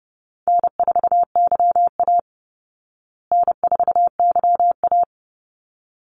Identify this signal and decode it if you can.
Morse code, heard as a keyed tone, about 30 wpm, 700 Hz. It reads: N4YA N4YA
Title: Easy Call Signs Repeated 2x - 30wpm